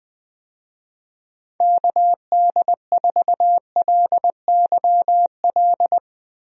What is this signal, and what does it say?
Morse code, heard as a keyed tone, about 20 wpm, 700 Hz. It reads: KD4LYL
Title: USA Call Signs - 6 Characters - 20wpm